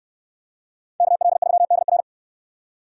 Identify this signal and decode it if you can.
Morse code, heard as a keyed tone, about 69 wpm, 700 Hz. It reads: KL3DR